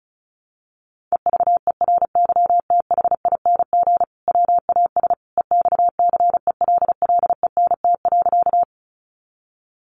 Morse code, heard as a keyed tone, about 34 wpm, 700 Hz. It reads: EVERYTHING WAS EXCELLENT.
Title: Sentences from Top 1000 Words 35wpm